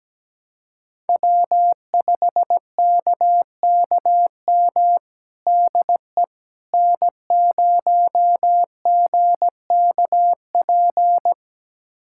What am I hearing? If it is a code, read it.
Morse code, heard as a keyed tone, about 17 wpm, 700 Hz. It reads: W5KKM DE N0GKP
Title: USA Callsign DE Callsign - 5 Characters - 17wpm